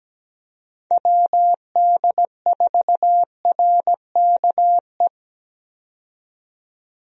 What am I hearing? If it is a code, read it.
Morse code, heard as a keyed tone, about 17 wpm, 700 Hz. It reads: WD4RKE
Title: USA Call Signs - 6 Characters - 17wpm